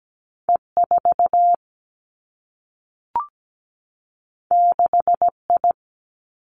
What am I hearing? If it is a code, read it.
Morse code, heard as a keyed tone, about 17 wpm, 700 Hz. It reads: E 4 6 I
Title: Two Challenging Letter-Number Combinations 17wpm